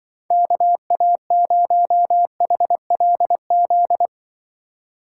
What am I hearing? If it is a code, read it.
Morse code, heard as a keyed tone, about 25 wpm, 700 Hz. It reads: KA0HLZ
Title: Speed Racing - USA 2x3 Call Signs 20wpm